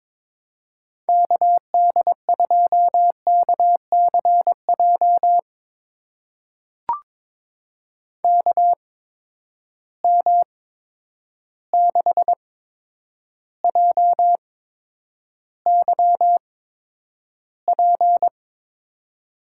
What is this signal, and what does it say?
Morse code, heard as a keyed tone, about 22 wpm, spaced as for 9 wpm, 700 Hz. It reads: KD2KCJ KM6JYP
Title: USA Call Signs - 6 Characters - 8x Character Spacing - 22wpm